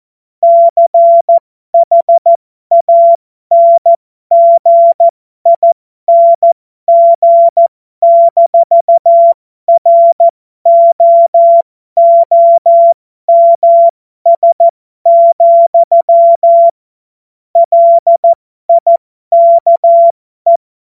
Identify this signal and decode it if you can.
Morse code, heard as a keyed tone, about 14 wpm, spaced as for 12 wpm, 680 Hz. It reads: CHANGING-ROOMS, LIKE
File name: Morsekurs / Witze